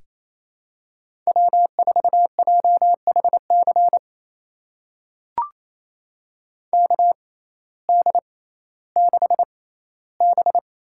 Morse code, heard as a keyed tone, about 28 wpm, spaced as for 15 wpm, 700 Hz. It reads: W4JHC KD6B
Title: USA Call Signs - 6x Character Spacing - 28wpm